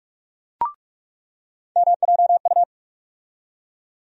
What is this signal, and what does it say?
Morse code, heard as a keyed tone, about 46 wpm, 700 Hz. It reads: MJU
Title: USA Call Sign Postfixes - 3 Letters 45wpm